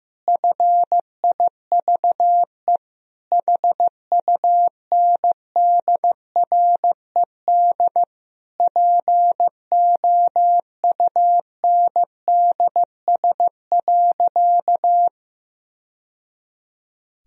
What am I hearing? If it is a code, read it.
Morse code, heard as a keyed tone, about 15 wpm, 700 Hz. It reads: FIVE HUNDRED POUNDS.
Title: Sentences from Top 1000 Words - Review Part 4 - 15wpm